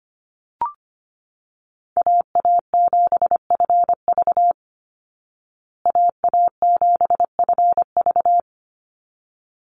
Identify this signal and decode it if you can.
Morse code, heard as a keyed tone, about 25 wpm, 700 Hz. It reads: AA7FV AA7FV
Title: Call Signs Repeated 3x - 25wpm